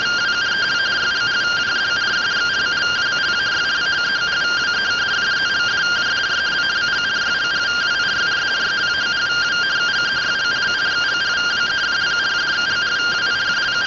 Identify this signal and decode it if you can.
Two-tone teleprinter signal, about 170 Hz shift.